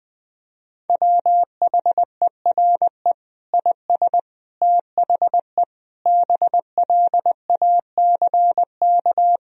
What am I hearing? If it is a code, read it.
Morse code, heard as a keyed tone, about 20 wpm, 700 Hz. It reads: WHERE IS THE BLACK
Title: Sentences from Top 1000 Words - Review Part 2 - 20wpm